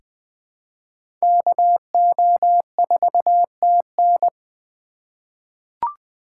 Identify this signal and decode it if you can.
Morse code, heard as a keyed tone, about 20 wpm, 700 Hz. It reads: KO4TN
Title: USA Call Signs - 5 Characters - 3x Character Spacing - 20wpm